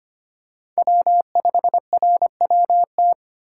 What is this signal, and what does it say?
Morse code, heard as a keyed tone, about 25 wpm, 700 Hz. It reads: W5RWT
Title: USA Call Signs - 5 Characters - 3x Character Spacing - 25wpm